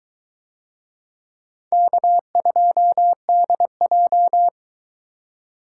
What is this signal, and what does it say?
Morse code, heard as a keyed tone, about 22 wpm, 700 Hz. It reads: K2DJ